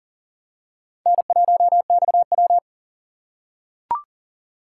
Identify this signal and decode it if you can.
Morse code, heard as a keyed tone, about 41 wpm, 700 Hz. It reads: N1XW
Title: USA Call Signs - 4 Characters - 4x Character Spacing - 40wpm